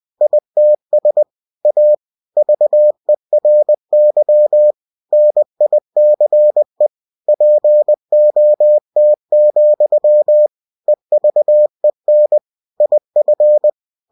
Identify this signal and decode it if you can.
Morse code, heard as a keyed tone, about 20 wpm, 600 Hz. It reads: ITS A VERY NICE POT, EVEN IF
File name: cw-books / winnie-the-pooh / winnie-20wpm-page-0050